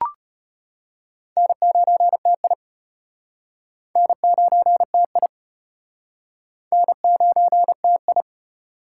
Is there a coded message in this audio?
Yes: Morse code from 1.4 to 8.2 s, heard as a keyed tone, about 35 wpm, 700 Hz. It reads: N9TI N9TI N9TI